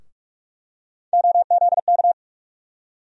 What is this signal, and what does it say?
Morse code, heard as a keyed tone, about 43 wpm, 700 Hz. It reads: OZK
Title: USA Call Sign Postfixes - 3 Letters 45wpm